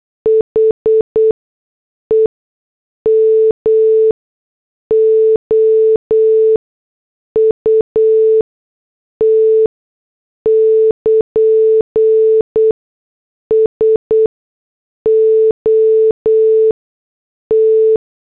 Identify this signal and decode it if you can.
Morse code, heard as a keyed tone, about 8 wpm, spaced as for 6 wpm, 440 Hz. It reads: HEM OUT <KN> SO T
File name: morse / headlines